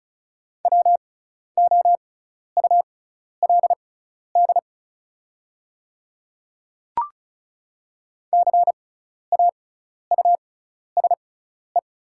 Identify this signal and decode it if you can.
Morse code, heard as a keyed tone, about 36 wpm, spaced as for 12 wpm, 700 Hz. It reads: WOULD CAUSE
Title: 6x Character Spacing - Top 200 Words - 35wpm